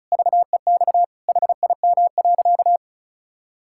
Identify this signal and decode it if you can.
Morse code, heard as a keyed tone, about 36 wpm, 700 Hz. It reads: VEX HIM.
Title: Pangram Challenge v2 - 35wpm